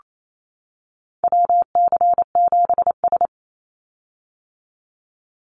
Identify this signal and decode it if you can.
Morse code, heard as a keyed tone, about 28 wpm, 700 Hz. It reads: WC7S
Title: Call Signs - 28wpm